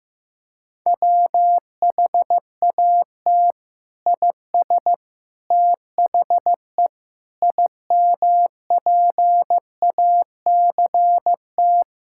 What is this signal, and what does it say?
Morse code, heard as a keyed tone, about 16 wpm, 700 Hz. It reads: WHAT IS THE IMPACT
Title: Sentences from Top 1000 Words - Review Part 6 - 15wpm